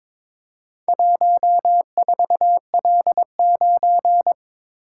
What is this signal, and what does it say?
Morse code, heard as a keyed tone, about 22 wpm, 700 Hz. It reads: 14L9
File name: practice / 22wpm / Four Letter-Number Combinations - 22wpm